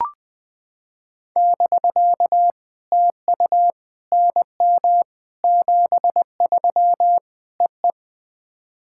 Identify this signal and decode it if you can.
Morse code, heard as a keyed tone, about 20 wpm, 700 Hz. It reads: <BK> TU NM 73 EE
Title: POTA QSO Practice - 20wpm